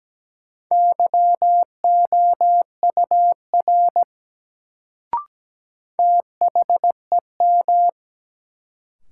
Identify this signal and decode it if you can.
Morse code, heard as a keyed tone, about 17 wpm, 700 Hz. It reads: YOUR THEM
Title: Sentences from Top 100 Words - New Word Drill - 17wpm